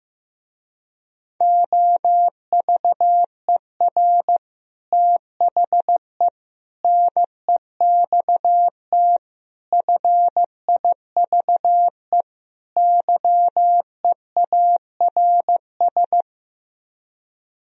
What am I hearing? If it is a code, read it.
Morse code, heard as a keyed tone, about 15 wpm, 700 Hz. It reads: OVER THE NEXT FIVE YEARS
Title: Speed-Racing - Sets of 5 Words 15wpm